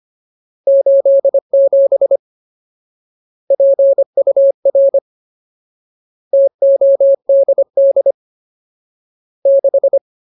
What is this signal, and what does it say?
Morse code, heard as a keyed tone, about 25 wpm, 550 Hz. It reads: 87 PUR TODD 6
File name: Ultimate Word List / 25WPM Ultimate